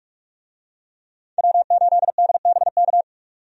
Morse code, heard as a keyed tone, about 41 wpm, 700 Hz. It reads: W8DBK
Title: USA Call Signs - 5 Characters - 45wpm